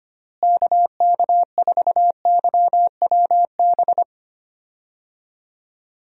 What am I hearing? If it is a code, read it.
Morse code, heard as a keyed tone, about 25 wpm, 700 Hz. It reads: KK4YWB
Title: USA Call Signs - 6 Characters - Repeated 3x - 25wpm